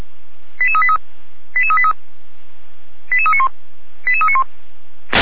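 Five-tone selective call (ZVEI), heard as quick runs of five tones, tones of about 70 ms. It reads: 88383 88383 88391 88391
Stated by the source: ZVEI five-tone call received on SDR, from feuerwehr-orlamuende fire brigade audio server.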